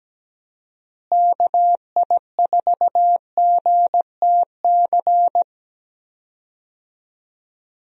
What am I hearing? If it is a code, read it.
Morse code, heard as a keyed tone, about 17 wpm, 700 Hz. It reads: KI4GTC
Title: USA Call Signs - 6 Characters - Repeated 2x - 17wpm